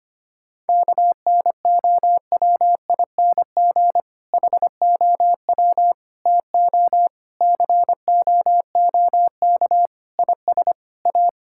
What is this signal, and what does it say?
Morse code, heard as a keyed tone, about 25 wpm, 700 Hz. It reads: KNOWING HOW TO COOK IS A
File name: practice / 25wpm / Difficult Sentences with Hints - 25wpm